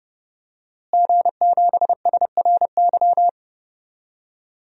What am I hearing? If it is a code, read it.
Morse code, heard as a keyed tone, about 30 wpm, 700 Hz. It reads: G7SRY
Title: DXCC-100 Call Signs - 5 Characters - Repeated 2x - 30wpm